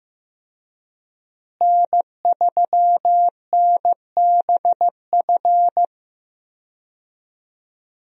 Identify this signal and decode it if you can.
Morse code, heard as a keyed tone, about 14 wpm, 700 Hz. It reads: N3NBF
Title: USA Call Signs - 5 Characters - Repeated 2x - 15wpm